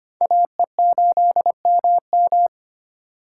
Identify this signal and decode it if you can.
Morse code, heard as a keyed tone, about 34 wpm, spaced as for 30 wpm, 700 Hz. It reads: AE8MM
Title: Speed Racing - USA 2x2 Call Signs 25wpm